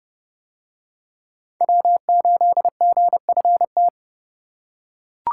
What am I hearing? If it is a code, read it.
Morse code, heard as a keyed tone, about 26 wpm, 700 Hz. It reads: W8GFT